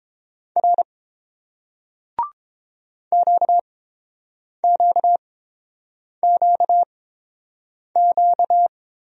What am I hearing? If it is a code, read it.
Morse code, heard as a keyed tone, about 30 wpm, spaced as for 7 wpm, 700 Hz. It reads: R QQQQ